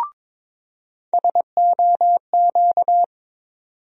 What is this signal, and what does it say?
Morse code, heard as a keyed tone, about 22 wpm, 700 Hz. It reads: SOQ
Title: USA Call Sign Postfixes - 3 Letters 22wpm